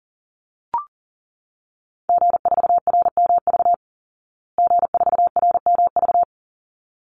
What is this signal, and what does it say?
Morse code, heard as a keyed tone, about 37 wpm, 700 Hz. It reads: G4RMV G4RMV
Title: Call Signs Repeated 3x - 40wpm